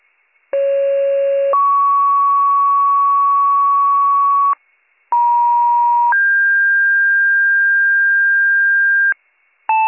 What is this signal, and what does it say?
Two-tone page, heard as a one-second tone, then a three-second tone, about 550 then 1120 Hz, about 950 then 1600 Hz, with noise. It is heard from 0.5 to 9.1 s.